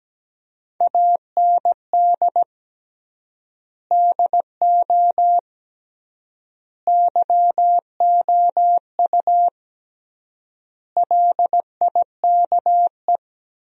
Morse code, heard as a keyed tone, about 17 wpm, 700 Hz. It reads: AND DO YOU LIKE
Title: 3x Word-Spacing - Sentences from Top 100 Words - 17wpm